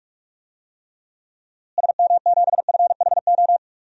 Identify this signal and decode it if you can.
Morse code, heard as a keyed tone, about 47 wpm, 700 Hz. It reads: SM7FHO
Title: DXCC-100 Call Signs - 6 Characters - Repeated 3x - 45wpm